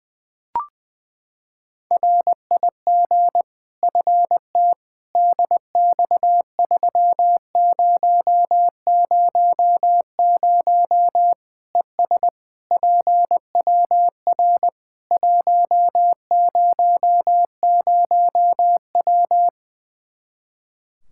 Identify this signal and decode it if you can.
Morse code, heard as a keyed tone, about 21 wpm, 700 Hz. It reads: RIG FT DX3000 ES PWR 100W
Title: K1USN SST QSO Practice 20wpm